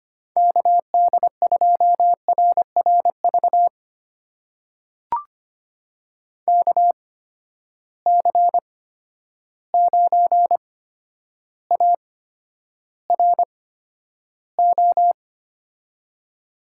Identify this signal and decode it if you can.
Morse code, heard as a keyed tone, about 25 wpm, spaced as for 11 wpm, 700 Hz. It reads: KD2RRV KC9ARO
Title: USA Call Signs - 6 Characters - 8x Character Spacing - 25wpm